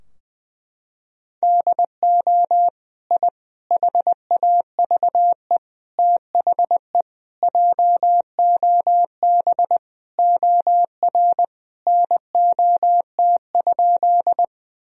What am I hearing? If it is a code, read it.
Morse code, heard as a keyed tone, about 21 wpm, 700 Hz. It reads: DO I HAVE THE JOB OR NOT?
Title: Sentences from Top 1000 Words - Review Part 3 - 20wpm